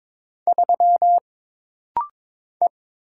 Morse code, heard as a keyed tone, about 22 wpm, 700 Hz. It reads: 3 E
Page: This rapid-fire timing is geared to push your ICR (Instant Character Recognition) skills.